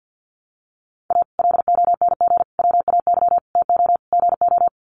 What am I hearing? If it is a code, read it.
Morse code, heard as a keyed tone, about 53 wpm, 700 Hz. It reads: A LONG WAY TO GO
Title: Sets of 5 Words 50wpm